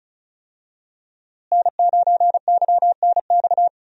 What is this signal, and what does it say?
Morse code, heard as a keyed tone, about 36 wpm, 700 Hz. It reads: N9YNX
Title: USA Call Signs - 5 Characters - Repeated 3x - 35wpm